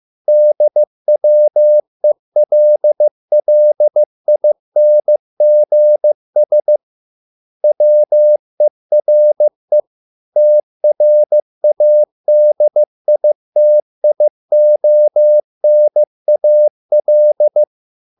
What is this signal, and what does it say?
Morse code, heard as a keyed tone, about 15 wpm, 600 Hz. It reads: DWELLINGS WERE TRADITIONAL